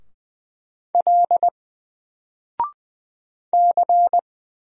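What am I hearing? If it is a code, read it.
Morse code, heard as a keyed tone, about 20 wpm, 700 Hz. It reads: L C
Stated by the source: Single letter and number Morse drill at 20 wpm, from N4LCD.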